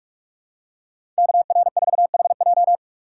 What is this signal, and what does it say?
Morse code, heard as a keyed tone, about 43 wpm, 700 Hz. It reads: KR4HJ
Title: USA Call Signs - 5 Characters - 4x Character Spacing - 45wpm